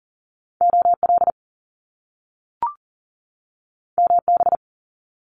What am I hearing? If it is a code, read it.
Morse code, heard as a keyed tone, about 39 wpm, 700 Hz. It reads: OL MB
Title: Top Two Letter Combinations in Words 40wpm